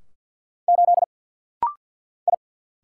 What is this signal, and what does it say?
Morse code, heard as a keyed tone, about 49 wpm, 700 Hz. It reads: 8 I
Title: Single Letter-Number - Rapid-Fire 50wpm